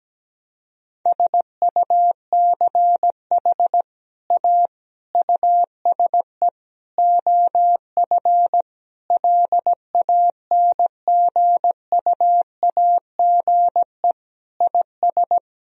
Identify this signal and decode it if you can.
Morse code, heard as a keyed tone, about 17 wpm, 700 Hz. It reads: SUCH A USE OF LANGUAGE IS
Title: Sentences from Top 1000 Words - Review Part 3 - 17wpm